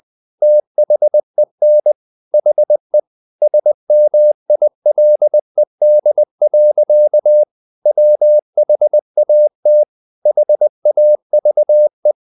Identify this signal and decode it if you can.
Morse code, heard as a keyed tone, about 20 wpm, 600 Hz. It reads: THEN HE SMILED. WHAT HAVE
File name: the old man and the sea / the old man and the sea part 1 20wpm